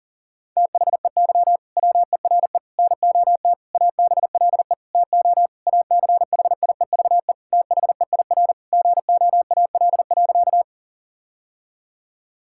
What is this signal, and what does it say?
Morse code, heard as a keyed tone, about 40 wpm, 700 Hz. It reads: THEY WERE NOT ABLE TO ACHIEVE THEIR GOAL.
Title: Sentences from Top 1000 Words - Review Part 6 - 40wpm